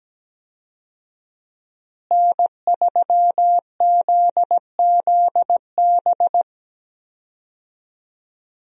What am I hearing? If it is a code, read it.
Morse code, heard as a keyed tone, about 17 wpm, 700 Hz. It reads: N3ZZB
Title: USA Call Signs - 5 Characters - Repeated 3x - 17wpm